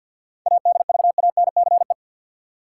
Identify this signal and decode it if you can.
Morse code, heard as a keyed tone, about 52 wpm, 700 Hz. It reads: ADVANCE
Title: Sentences from Top 1400 Words - New Word Drill Alt - 50wpm